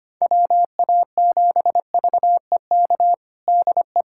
Morse code, heard as a keyed tone, about 25 wpm, 700 Hz. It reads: WA7VEK DE
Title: USA Callsign DE Callsign - 6 Characters - 25wpm